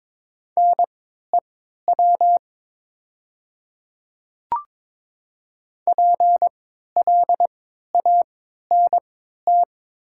Morse code, heard as a keyed tone, about 22 wpm, spaced as for 12 wpm, 700 Hz. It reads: NEW PLANT
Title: 3x Character Spacing - Top 300 Words - 22wpm